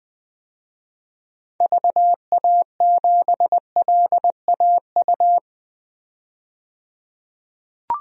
Morse code, heard as a keyed tone, about 20 wpm, 700 Hz. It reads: VA7LAU
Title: DXCC-100 Call Signs - 6 Characters - 20wpm